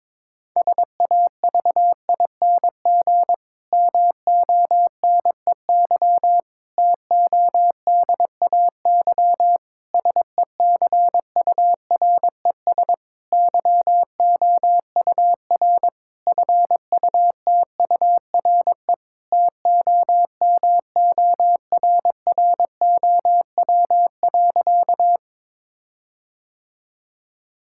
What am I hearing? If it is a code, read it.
Morse code, heard as a keyed tone, about 22 wpm, 700 Hz. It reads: SAVING MONEY TODAY SECURES YOUR FUTURE TOMORROW.
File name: Difficult Sentences with Hints - 22wpm